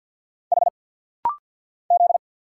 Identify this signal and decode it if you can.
Morse code, heard as a keyed tone, about 50 wpm, 700 Hz. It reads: H Z
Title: Single Letters - Rapid-Fire 50wpm